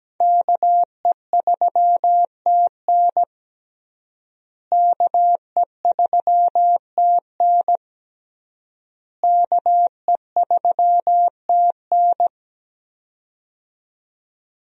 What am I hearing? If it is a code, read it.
Morse code, heard as a keyed tone, about 17 wpm, 700 Hz. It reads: KE3TN KE3TN KE3TN
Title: USA Call Signs - 5 Characters - Repeated 3x - 17wpm